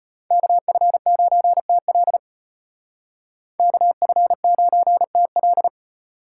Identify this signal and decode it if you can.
Morse code, heard as a keyed tone, about 32 wpm, 700 Hz. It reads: KF9TL KF9TL